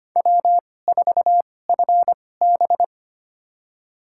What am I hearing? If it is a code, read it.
Morse code, heard as a keyed tone, about 25 wpm, spaced as for 18 wpm, 700 Hz. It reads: W4FB
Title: USA Call Signs - 4 Characters - 2x Character Spacing - 25wpm